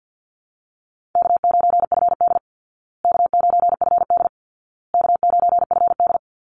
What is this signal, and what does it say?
Morse code, heard as a keyed tone, about 50 wpm, 700 Hz. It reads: K9FD K9FD K9FD
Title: Call Signs Repeated 2x - 50wpm